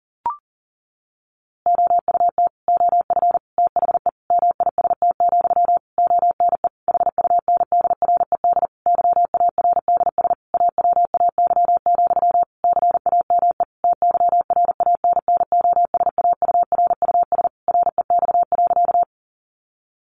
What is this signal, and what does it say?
Morse code, heard as a keyed tone, about 40 wpm, 700 Hz. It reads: OUT OF THE MIST, ONE HUNDRED YARDS AWAY, CAME TYRANNOSAURUS REX.
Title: A Sound Of Thunder 40wpm